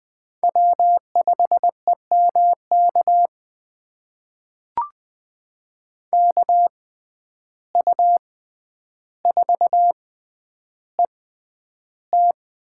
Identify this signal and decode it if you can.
Morse code, heard as a keyed tone, about 20 wpm, spaced as for 10 wpm, 700 Hz. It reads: W5EMK KU4ET
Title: USA Call Signs - 5 Characters - 6x Character Spacing - 20wpm